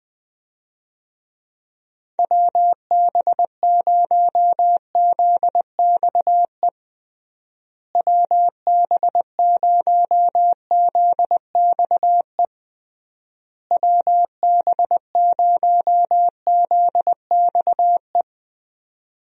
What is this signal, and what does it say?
Morse code, heard as a keyed tone, about 20 wpm, 700 Hz. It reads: WB0ZXE WB0ZXE WB0ZXE
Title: USA Call Signs - 6 Characters - Repeated 3x - 20wpm